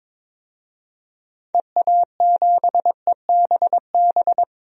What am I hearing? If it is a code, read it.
Morse code, heard as a keyed tone, about 22 wpm, 700 Hz. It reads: EA7EBB
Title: DXCC-100 Call Signs - 6 Characters - 22wpm